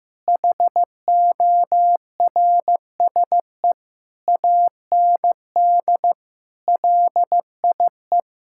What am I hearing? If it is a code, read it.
Morse code, heard as a keyed tone, about 16 wpm, 700 Hz. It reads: HORSE AND LIE
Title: Sentences from Top 1600 Words 15wpm